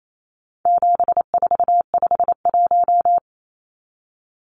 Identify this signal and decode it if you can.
Morse code, heard as a keyed tone, about 28 wpm, 700 Hz. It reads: 7451
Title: Numbers - 4-Digit 28wpm